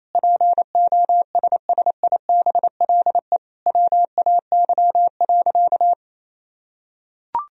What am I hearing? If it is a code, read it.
Morse code, heard as a keyed tone, about 28 wpm, 700 Hz. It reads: POSSIBLE WAY.
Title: Sentences from Top 500 Words and Abbreviations 28wpm